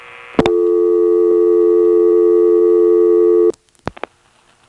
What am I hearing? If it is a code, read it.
Dial tone (North America), 350 + 440 Hz.